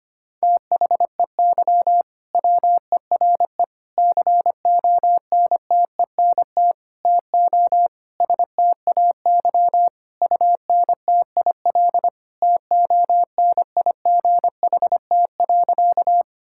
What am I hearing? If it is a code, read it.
Morse code, heard as a keyed tone, about 25 wpm, 700 Hz. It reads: THEY WERE CONTENT TO STAY UNTIL TONIGHT.
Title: Sentences from Top 1000 Words - Review Part 6 - 25wpm